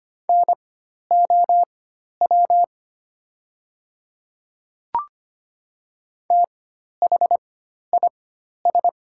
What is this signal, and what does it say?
Morse code, heard as a keyed tone, about 25 wpm, spaced as for 12 wpm, 700 Hz. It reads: NOW THIS
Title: 4x Character Spacing - Top 200 Words - 25wpm